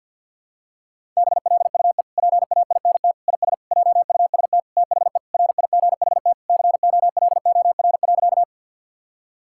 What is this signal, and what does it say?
Morse code, heard as a keyed tone, about 50 wpm, 700 Hz. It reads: BLUE PAINT IS JUST THE RIGHT COLOR.
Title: Sentences from Top 1000 Words - Review Part 6 - 50wpm